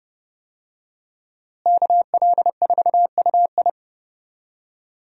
Morse code, heard as a keyed tone, about 30 wpm, 700 Hz. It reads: KL4UI
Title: USA Call Signs - 5 Characters - Repeated 3x - 30wpm